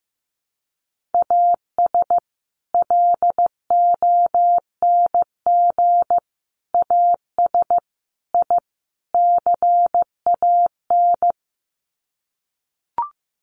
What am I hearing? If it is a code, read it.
Morse code, heard as a keyed tone, about 15 wpm, 700 Hz. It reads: AS LONG AS I CAN
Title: Sets of 5 Words Encore2 15wpm